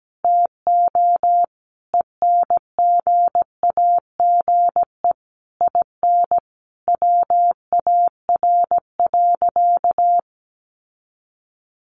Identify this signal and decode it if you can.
Morse code, heard as a keyed tone, about 17 wpm, 700 Hz. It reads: TO ENGAGE IN WAR.
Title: Sentences from Top 1500 Words 17wpm